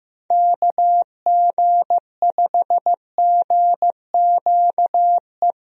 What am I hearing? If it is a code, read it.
Morse code, heard as a keyed tone, about 15 wpm, 700 Hz. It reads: KG5GQE
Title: USA Call Signs - 6 Characters - Repeated 3x - 15wpm